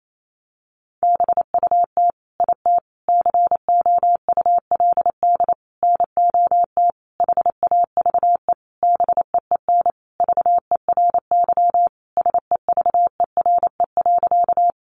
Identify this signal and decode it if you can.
Morse code, heard as a keyed tone, about 28 wpm, 700 Hz. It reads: BUT IT COULD NOT HAVE BEEN VERY SEVERE.
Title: Sentences from Top 1400 Words 28wpm